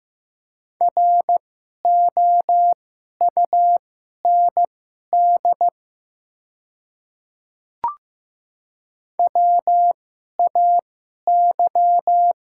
Morse code, heard as a keyed tone, about 15 wpm, spaced as for 11 wpm, 700 Hz. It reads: ROUND WAY
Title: 2x Character Spacing - Top 200 Words - 15wpm